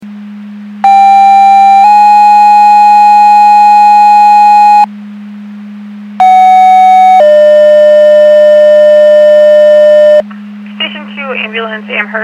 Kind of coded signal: two-tone page